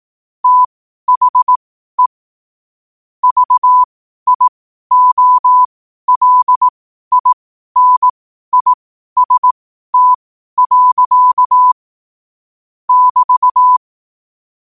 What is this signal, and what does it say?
Morse code, heard as a keyed tone, about 18 wpm, spaced as for 12 wpm, 1000 Hz. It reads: THE VIOLINIST. <BT>